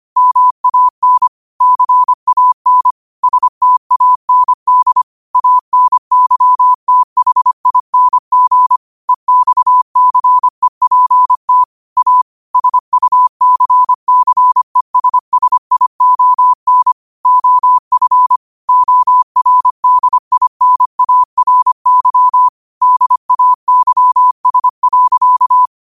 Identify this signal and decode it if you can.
Morse code, heard as a keyed tone, about 25 wpm, 1000 Hz. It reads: MAN CAN STAND ANYTHING EXCEPT A SUCCESSION OF ORDINARY DAYS.
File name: qotdfiles / 25wpm